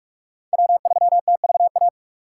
Morse code, heard as a keyed tone, about 41 wpm, 700 Hz. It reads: W3TVA